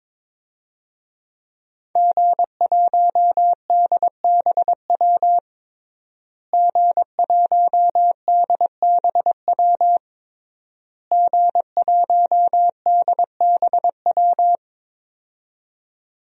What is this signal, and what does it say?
Morse code, heard as a keyed tone, about 22 wpm, 700 Hz. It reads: G1DBW G1DBW G1DBW
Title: DXCC-100 Call Signs - 5 Characters - Repeated 3x - 22wpm